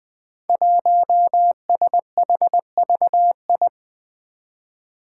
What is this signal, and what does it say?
Morse code, heard as a keyed tone, about 21 wpm, 700 Hz. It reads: 1SHVI
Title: Five Challenging Letter-Number Combinations 20wpm